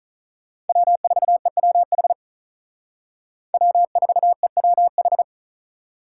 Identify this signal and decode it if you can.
Morse code, heard as a keyed tone, about 43 wpm, 700 Hz. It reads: W4EWH W4EWH
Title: Speed Racing - USA 1x3 Call Signs 35wpm